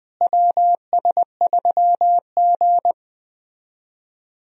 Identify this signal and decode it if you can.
Morse code, heard as a keyed tone, about 21 wpm, 700 Hz. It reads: WS3G
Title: USA Call Signs - 4 Characters - Repeated 2x - 20wpm